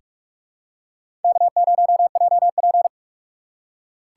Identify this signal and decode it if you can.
Morse code, heard as a keyed tone, about 48 wpm, 690 Hz. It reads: K0JP
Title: USA Call Signs - 4 Characters - 3x Character Spacing - 45wpm